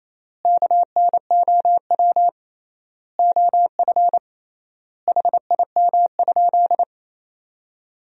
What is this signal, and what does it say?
Morse code, heard as a keyed tone, about 28 wpm, 700 Hz. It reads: KNOW OF HIM?
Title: 3x Word-Spacing - Sentences from Top 100 Words - 28wpm